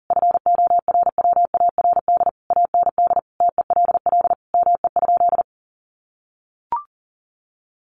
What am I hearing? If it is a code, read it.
Morse code, heard as a keyed tone, about 40 wpm, 700 Hz. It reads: FORWARD AND TELL ME?
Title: Sentences from Top 1300 Words 40wpm